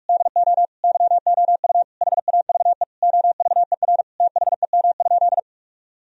Morse code, heard as a keyed tone, about 46 wpm, 700 Hz. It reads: DO YOU HAVE OVER THEM?
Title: Sentences from Top 1000 Words - Review Part 3 - 45wpm